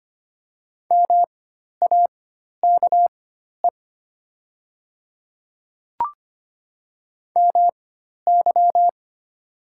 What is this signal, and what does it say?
Morse code, heard as a keyed tone, about 25 wpm, spaced as for 12 wpm, 690 Hz. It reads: MAKE MY
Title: 4x Character Spacing - Top 200 Words - 25wpm